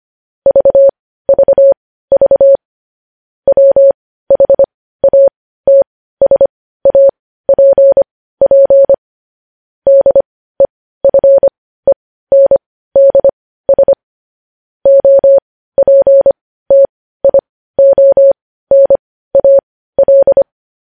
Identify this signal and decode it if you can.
Morse code, heard as a keyed tone, about 25 wpm, spaced as for 15 wpm, 550 Hz. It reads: VVV WHATSAPP DEFENDS OPTIONAL
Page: news headlines in morse code at 15 wpm